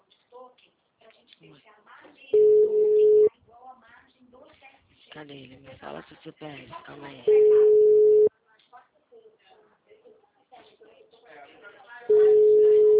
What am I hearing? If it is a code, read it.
Ringback tone (Europe, 425 Hz), with speech.